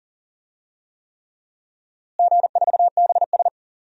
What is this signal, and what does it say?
Morse code, heard as a keyed tone, about 40 wpm, 700 Hz. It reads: G4BS
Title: DXCC-100 Call Signs - 4 Characters - Repeated 3x - 40wpm